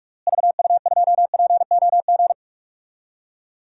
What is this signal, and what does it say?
Morse code, heard as a keyed tone, about 41 wpm, 700 Hz. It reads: VU2POG
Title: DXCC-100 Call Signs - 6 Characters - Repeated 3x - 45wpm